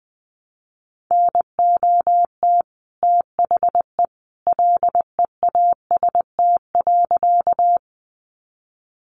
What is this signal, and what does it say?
Morse code, heard as a keyed tone, about 20 wpm, 700 Hz. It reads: NOT THE LEAST.
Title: Sentences from Top 700 Words 20wpm